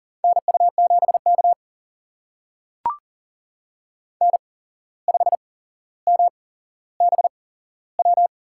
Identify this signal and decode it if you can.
Morse code, heard as a keyed tone, about 40 wpm, spaced as for 16 wpm, 700 Hz. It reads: NU7K N5MBW
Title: USA Call Signs - 8x Character Spacing - 40wpm